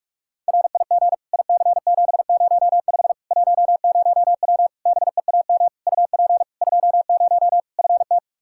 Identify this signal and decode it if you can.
Morse code, heard as a keyed tone, about 41 wpm, 700 Hz. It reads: RIG IC705 10W BEAM UP 20 FT
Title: Practice QSO - Exchange 2 - Info - 45wpm